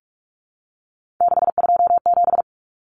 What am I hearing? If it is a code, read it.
Morse code, heard as a keyed tone, about 44 wpm, 700 Hz. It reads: 627
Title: Numbers - 3-Digit 45wpm